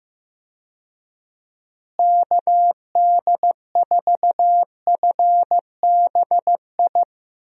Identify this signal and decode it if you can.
Morse code, heard as a keyed tone, about 15 wpm, 700 Hz. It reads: KD4FBI
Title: USA Call Signs - 6 Characters - Repeated 3x - 15wpm